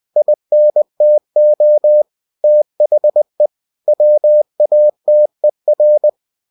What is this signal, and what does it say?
Morse code, heard as a keyed tone, about 20 wpm, 600 Hz. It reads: INTO THE WATER
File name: cw-books / winnie-the-pooh / winnie-20wpm-page-0095